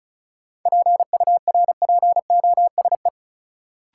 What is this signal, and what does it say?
Morse code, heard as a keyed tone, about 36 wpm, 700 Hz. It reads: PURPOSE
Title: Sentences from Top 800 Words - New Word Drill - 35wpm